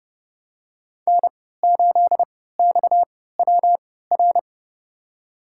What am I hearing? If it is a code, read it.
Morse code, heard as a keyed tone, about 30 wpm, spaced as for 17 wpm, 700 Hz. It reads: N8XWR
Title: USA Call Signs - 5 Characters - 3x Character Spacing - 30wpm